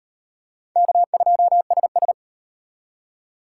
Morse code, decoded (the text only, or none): K2SS